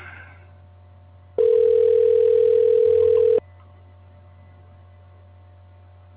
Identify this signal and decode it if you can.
Ringback tone (North America), 440 + 480 Hz.